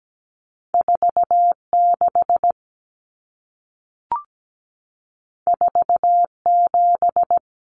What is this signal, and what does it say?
Morse code, heard as a keyed tone, about 17 wpm, 700 Hz. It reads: 46 47
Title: Numbers - 2-Digit 17wpm